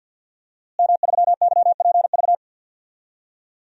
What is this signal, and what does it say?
Morse code, heard as a keyed tone, about 53 wpm, 700 Hz. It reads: M3YPV
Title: DXCC-100 Call Signs - 5 Characters - Repeated 2x - 50wpm